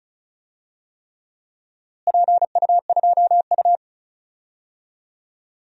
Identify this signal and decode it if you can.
Morse code, heard as a keyed tone, about 34 wpm, 700 Hz. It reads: PU2U
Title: DXCC-100 Call Signs - 4 Characters - Repeated 3x - 35wpm